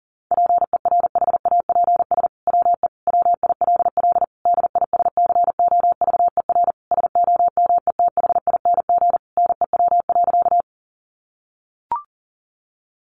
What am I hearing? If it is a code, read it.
Morse code, heard as a keyed tone, about 40 wpm, 700 Hz. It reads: PERHAPS WE WILL DISCOVER SOMETHING NEW.
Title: Sentences from Top 1100 Words 40wpm